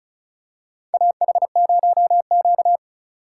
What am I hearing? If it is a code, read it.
Morse code, heard as a keyed tone, about 36 wpm, 700 Hz. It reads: AH0Q
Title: USA Call Signs - 4 Characters - 6x Character Spacing - 35wpm